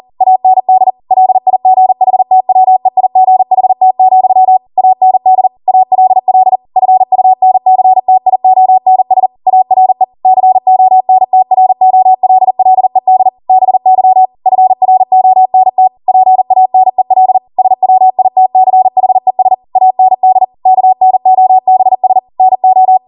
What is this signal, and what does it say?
Morse code, heard as a keyed tone, about 40 wpm, 750 Hz. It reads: AND LIGHTWEIGHT, AND ALL FUNCTIONS ARE CONTROLLED BY FRONT PANEL SWITCHES AND KNOBS NO
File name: Morse / 40WPM